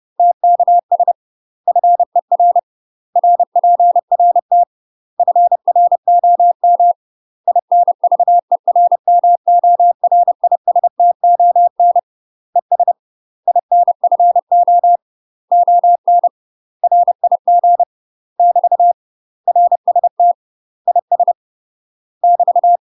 Morse code, heard as a keyed tone, about 30 wpm, 700 Hz. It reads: TKS FER RPRT FROM INVERMORISTON ES INFO ON RIG <BT> RST IS <BT>
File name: UK MASTER QSOs TRAINING / 30WPM / UK MASTER QSOs 0039